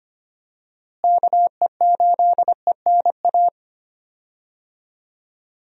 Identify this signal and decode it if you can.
Morse code, heard as a keyed tone, about 25 wpm, 700 Hz. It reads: KE8ENA
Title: USA Call Signs - 6 Characters - Repeated 3x - 25wpm